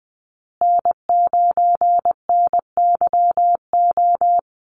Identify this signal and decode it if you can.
Morse code, heard as a keyed tone, about 20 wpm, 700 Hz. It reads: N9NYO
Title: USA Call Signs Repeated 2x 20wpm